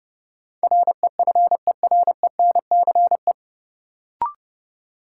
Morse code, heard as a keyed tone, about 30 wpm, 700 Hz. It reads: REFERENCE
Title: Sentences from Top 1200 Words - New Word Drill - 30wpm